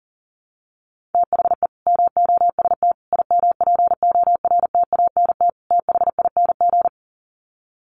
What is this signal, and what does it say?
Morse code, heard as a keyed tone, about 40 wpm, 700 Hz. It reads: THE MOST IMPORTANT THING
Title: Sets of 4 Words 40wpm